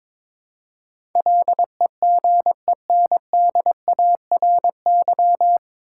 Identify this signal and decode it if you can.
Morse code, heard as a keyed tone, about 22 wpm, 700 Hz. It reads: LEGENDARY
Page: Morse Code Ninja practice 22wpm